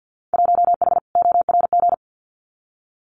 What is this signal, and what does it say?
Morse code, heard as a keyed tone, about 50 wpm, 700 Hz. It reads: 1H ORG
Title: Winter Field Day Exchanges 50wpm